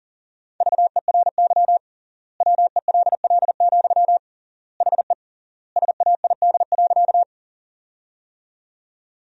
Morse code, heard as a keyed tone, about 40 wpm, 700 Hz. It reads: VERY WELL, HE SAID.
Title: 3x Word-Spacing - Sentences from Top 200 Words - 40wpm